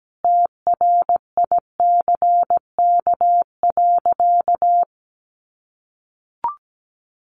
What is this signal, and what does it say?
Morse code, heard as a keyed tone, about 17 wpm, 700 Hz. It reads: TRICK.
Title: Sentences from Top 1500 Words 17wpm